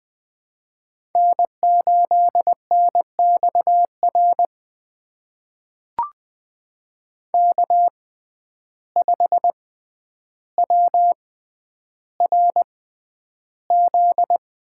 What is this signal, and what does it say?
Morse code, heard as a keyed tone, about 20 wpm, spaced as for 10 wpm, 700 Hz. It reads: N8NXR K5WRZ
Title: USA Call Signs - 5 Characters - 6x Character Spacing - 20wpm